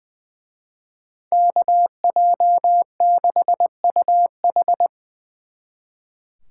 Morse code, heard as a keyed tone, about 24 wpm, 700 Hz. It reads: KJ6UH
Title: Speed Racing - USA 2x2 Call Signs 20wpm